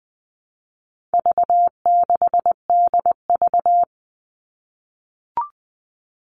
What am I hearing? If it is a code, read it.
Morse code, heard as a keyed tone, about 20 wpm, 700 Hz. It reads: V6DV
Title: All Cut Numbers - 4-Digit 20wpm